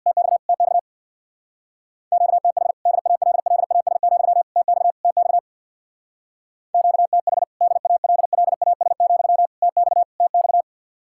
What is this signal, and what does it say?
Morse code, heard as a keyed tone, about 66 wpm, 700 Hz. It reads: TX TX QTH DALLAS, TX TX QTH DALLAS, TX TX